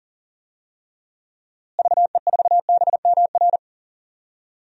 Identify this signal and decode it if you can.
Morse code, heard as a keyed tone, about 40 wpm, 700 Hz. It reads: VE4BMR
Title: DXCC-100 Call Signs - 6 Characters - Repeated 3x - 40wpm